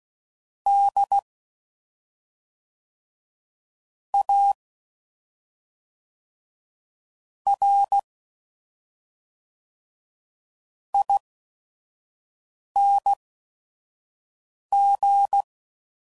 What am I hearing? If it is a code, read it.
Morse code, heard as a keyed tone, about 16 wpm, spaced as for 5 wpm, 800 Hz. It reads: D A R ING
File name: Suffixes / Special ING's / SlowToFaster ING's